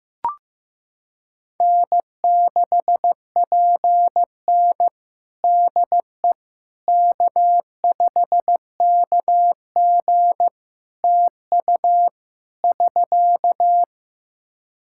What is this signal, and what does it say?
Morse code, heard as a keyed tone, about 15 wpm, 700 Hz. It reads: N6PN DE K5KG TU <SK>
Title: Practice QSO - Exchange 4 - Ending - 15wpm